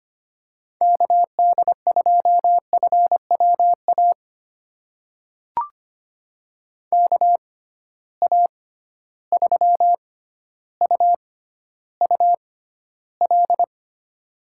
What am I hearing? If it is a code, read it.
Morse code, heard as a keyed tone, about 25 wpm, spaced as for 13 wpm, 700 Hz. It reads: KD2FWA KA3UUL
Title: USA Call Signs - 6 Characters - 6x Character Spacing - 25wpm